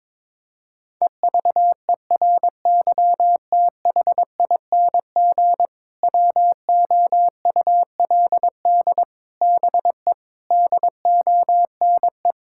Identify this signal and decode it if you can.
Morse code, heard as a keyed tone, about 22 wpm, 700 Hz. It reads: EVERYTHING WOULD BE DONE
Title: Sentences from Top 1000 Words - Review Part 5 - 22wpm